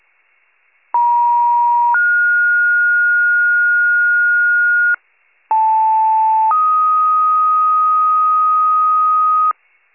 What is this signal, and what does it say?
Two-tone page, heard as a one-second tone, then a three-second tone, about 950 then 1430 Hz, about 880 then 1250 Hz, with noise. It is heard from 0.9 to 9.5 s.
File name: pager / 76 FIRE / wcalert